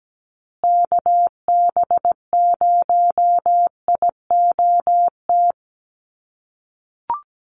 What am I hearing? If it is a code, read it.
Morse code, heard as a keyed tone, about 17 wpm, 700 Hz. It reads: KB0IOT
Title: USA Call Signs Repeated 2x 17wpm